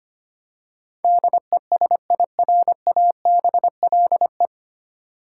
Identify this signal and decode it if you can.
Morse code, heard as a keyed tone, about 25 wpm, 690 Hz. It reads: DESIRABLE
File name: practice / 25wpm / 9 Letter Words - First 3 Letters Given - 25wpm